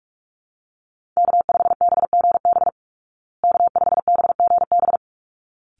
Morse code, heard as a keyed tone, about 44 wpm, 700 Hz. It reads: K5BGB K5BGB
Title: Call Signs Repeated 3x - 45wpm